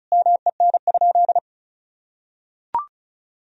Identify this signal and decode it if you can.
Morse code, heard as a keyed tone, about 36 wpm, 700 Hz. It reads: MEN?
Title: Sentences from Top 1000 Words - Review Part 2 - 35wpm